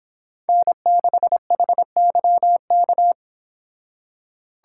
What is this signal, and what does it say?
Morse code, heard as a keyed tone, about 29 wpm, 700 Hz. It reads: N6HYK